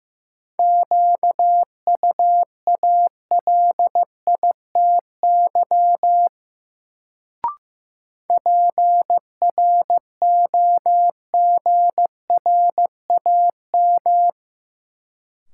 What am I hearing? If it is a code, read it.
Morse code, heard as a keyed tone, about 16 wpm, 700 Hz. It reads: QUALITY PROGRAM
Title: Sentences from Top 1500 Words - New Word Drill Alt - 15wpm